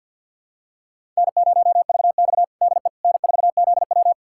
Morse code, heard as a keyed tone, about 52 wpm, 700 Hz. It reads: N0VX DE N4ZW
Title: USA Callsign DE Callsign - 4 Characters - 50wpm